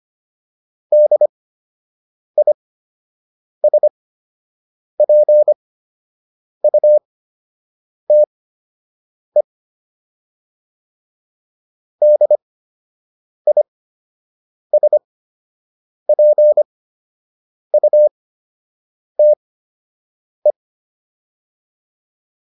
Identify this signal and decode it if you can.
Morse code, heard as a keyed tone, about 25 wpm, spaced as for 7 wpm, 600 Hz. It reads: DISPUTE DISPUTE
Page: fundamental-practice-files